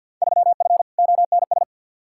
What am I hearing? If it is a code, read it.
Morse code, heard as a keyed tone, about 53 wpm, 700 Hz. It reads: 3F ONS
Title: Field Day Exchanges - 50wpm